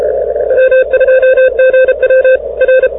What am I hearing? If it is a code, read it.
Morse code, heard as a keyed tone, about 33 wpm, 1550 Hz, with noise. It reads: M2GW R